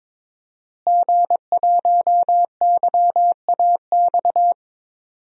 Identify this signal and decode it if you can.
Morse code, heard as a keyed tone, about 22 wpm, 700 Hz. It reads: G1YAX
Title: DXCC-100 Call Signs - 5 Characters - Repeated 3x - 22wpm